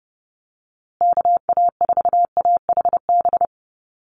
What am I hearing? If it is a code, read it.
Morse code, heard as a keyed tone, about 30 wpm, 700 Hz. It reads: KA4AHB
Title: USA Call Signs Repeated 2x 30wpm